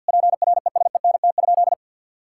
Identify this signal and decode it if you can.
Morse code, heard as a keyed tone, about 52 wpm, 700 Hz. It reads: PRESENT?
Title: Sentences from Top 1000 Words - Review Part 6 - 50wpm